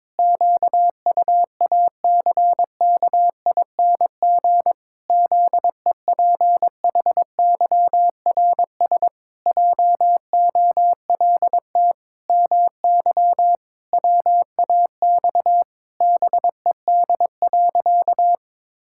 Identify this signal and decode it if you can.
Morse code, heard as a keyed tone, about 22 wpm, 700 Hz. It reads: QUACKING ZEPHYRS JOLT MY WAX BED.
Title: Pangram Challenge v2 - 22wpm